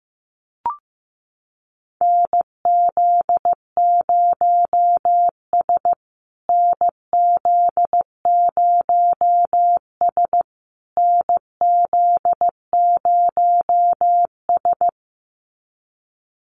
Morse code, heard as a keyed tone, about 14 wpm, 700 Hz. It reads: NZ0S NZ0S NZ0S
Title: USA Call Signs Repeated 3x 15wpm